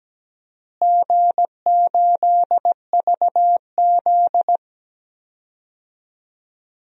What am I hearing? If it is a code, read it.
Morse code, heard as a keyed tone, about 17 wpm, 700 Hz. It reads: G8VZ
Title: DXCC-100 Call Signs - 4 Characters - Repeated 3x - 17wpm